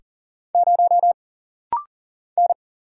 Morse code, heard as a keyed tone, about 40 wpm, 700 Hz. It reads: 0 N
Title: Single Letter-Number Alt - Rapid-Fire 40wpm